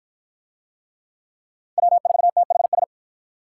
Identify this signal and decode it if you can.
Morse code, heard as a keyed tone, about 50 wpm, 700 Hz. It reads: W4THS